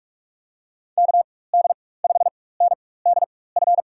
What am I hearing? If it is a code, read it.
Morse code, heard as a keyed tone, about 44 wpm, spaced as for 21 wpm, 700 Hz. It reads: KD5NDF
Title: USA Call Signs - 6 Characters - 4x Character Spacing - 45wpm